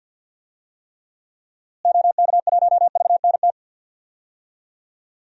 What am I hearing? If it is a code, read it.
Morse code, heard as a keyed tone, about 45 wpm, 700 Hz. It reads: OK1VNT
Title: DXCC-100 Call Signs - 6 Characters - Repeated 2x - 50wpm